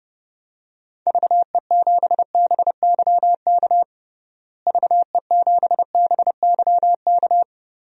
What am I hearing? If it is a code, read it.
Morse code, heard as a keyed tone, about 32 wpm, 700 Hz. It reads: VE7BYK VE7BYK
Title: DXCC-100 Call Signs - 6 Characters - Repeated 2x - 30wpm